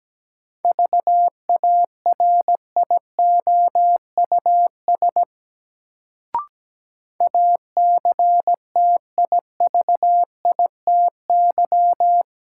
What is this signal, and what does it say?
Morse code, heard as a keyed tone, about 17 wpm, 700 Hz. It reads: VARIOUS ACTIVITY
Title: Sentences from Top 1000 Words - Review Part 4 - New Word Drill - 17wpm